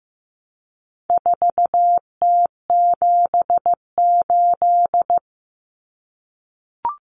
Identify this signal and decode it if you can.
Morse code, heard as a keyed tone, about 15 wpm, 700 Hz. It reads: 4T78
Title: Common Cut Numbers - 4-Digit 15wpm